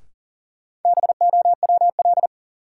Morse code, heard as a keyed tone, about 40 wpm, 700 Hz. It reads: BOWL
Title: Sentences from Top 1200 Words - New Word Drill - 40wpm